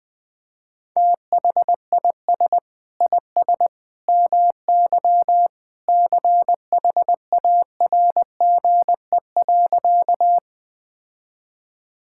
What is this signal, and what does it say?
Morse code, heard as a keyed tone, about 21 wpm, 700 Hz. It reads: THIS IS MY CHARGE.
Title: Sentences from Top 1000 Words - Review Part 6 - 20wpm